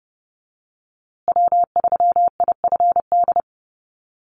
Morse code, heard as a keyed tone, about 30 wpm, 700 Hz. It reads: W3IFD
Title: USA Call Signs Repeated 2x 30wpm